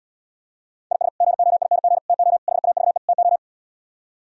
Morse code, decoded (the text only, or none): IS GOING TO HAVE TO